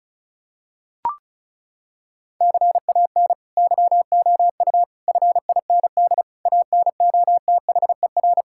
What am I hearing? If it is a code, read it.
Morse code, heard as a keyed tone, about 34 wpm, 700 Hz. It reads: CAN YOU FIND ANOTHER
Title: Sentences from Top 1600 Words 35wpm